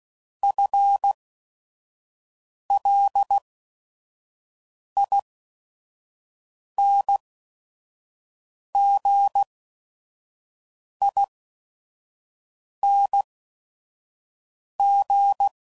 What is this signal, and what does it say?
Morse code, heard as a keyed tone, about 16 wpm, spaced as for 5 wpm, 800 Hz. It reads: FLINGING